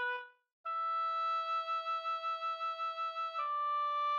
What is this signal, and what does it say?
There is no code.